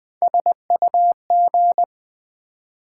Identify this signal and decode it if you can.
Morse code, heard as a keyed tone, about 20 wpm, 700 Hz. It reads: SUG
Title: Build A Word - 1 Letter At A Time - 5-Letter Words - 20wpm